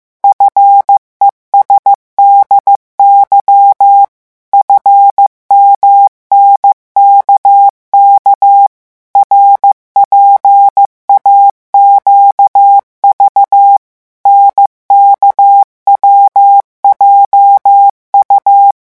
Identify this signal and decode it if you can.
Morse code, heard as a keyed tone, about 15 wpm, 800 Hz. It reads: FESDY FMNKK RPAQV NKWJU
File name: morse mp3 / revision7-vitesse 14 mots